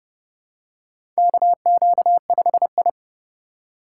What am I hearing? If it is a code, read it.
Morse code, heard as a keyed tone, about 31 wpm, 700 Hz. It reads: KQ5I
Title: USA Call Signs - 4 Characters - 3x Character Spacing - 30wpm